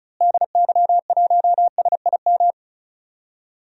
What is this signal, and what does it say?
Morse code, heard as a keyed tone, about 34 wpm, 700 Hz. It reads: DY1SIM
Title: DXCC-100 Call Signs - 6 Characters - Repeated 3x - 35wpm